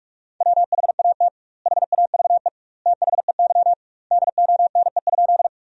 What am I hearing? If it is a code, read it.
Morse code, heard as a keyed tone, about 46 wpm, 700 Hz. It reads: WHAT HAVE THEY DONE?
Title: 2x Word-Spacing - Sentences from Top 500 Words - 45wpm